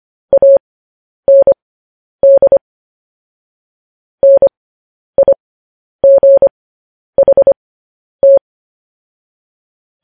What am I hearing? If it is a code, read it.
Morse code, heard as a keyed tone, about 25 wpm, spaced as for 10 wpm, 550 Hz. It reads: AND NIGHT
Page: news-headlines-in-morse-code-at-10-wpm